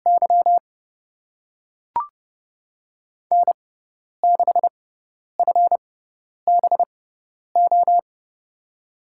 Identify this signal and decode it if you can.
Morse code, heard as a keyed tone, about 30 wpm, spaced as for 10 wpm, 700 Hz. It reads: Y N6FBO
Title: USA Call Signs - 5 Characters - 6x Character Spacing - 30wpm